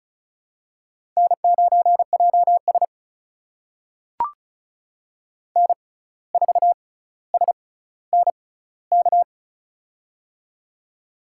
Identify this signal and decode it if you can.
Morse code, heard as a keyed tone, about 34 wpm, spaced as for 17 wpm, 700 Hz. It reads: N9JS N4SNK
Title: USA Call Signs - 6x Character Spacing - 35wpm